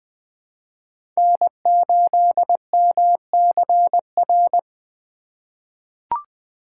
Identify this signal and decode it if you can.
Morse code, heard as a keyed tone, about 20 wpm, 700 Hz. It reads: N8MCR
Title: USA Call Signs - 4x Character Spacing - 20wpm